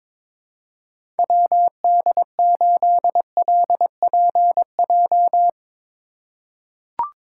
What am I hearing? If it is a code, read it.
Morse code, heard as a keyed tone, about 22 wpm, 700 Hz. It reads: WD8LPJ
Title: USA Call Signs - 6 Characters - 3x Character Spacing - 22wpm